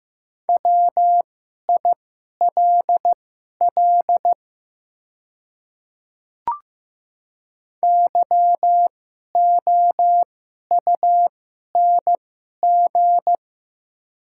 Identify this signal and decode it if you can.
Morse code, heard as a keyed tone, about 15 wpm, spaced as for 11 wpm, 700 Hz. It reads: WILL YOUNG
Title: 2x Character Spacing - Top 500 Words - 15wpm